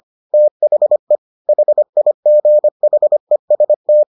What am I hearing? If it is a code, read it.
Morse code, heard as a keyed tone, about 25 wpm, 600 Hz. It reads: THE HIGHEST